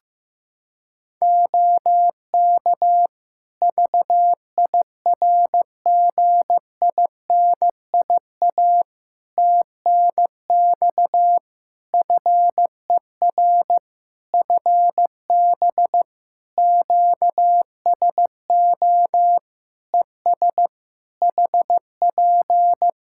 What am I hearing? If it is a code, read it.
Morse code, heard as a keyed tone, about 16 wpm, 700 Hz. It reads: OK VIRGINIA TNX FER FB QSO ES HP
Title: Practice QSO - Exchange 4 - Ending - 15wpm